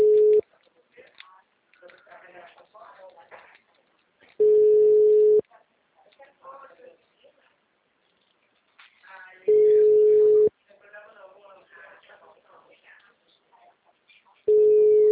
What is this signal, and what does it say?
Ringback tone (Europe, 425 Hz), with speech.